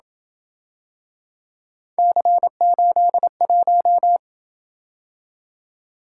Morse code, heard as a keyed tone, about 26 wpm, 700 Hz. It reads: C81